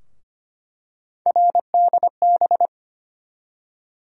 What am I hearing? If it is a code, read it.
Morse code, heard as a keyed tone, about 25 wpm, 700 Hz. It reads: RDB
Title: USA Call Sign Postfixes - 3 Letters 25wpm